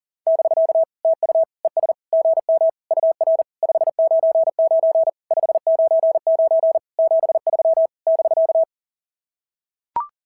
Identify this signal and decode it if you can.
Morse code, heard as a keyed tone, about 40 wpm, 690 Hz. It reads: <BK> TU ES GM UR 599 599 73 <BK>
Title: SOTA QSO Practice 40wpm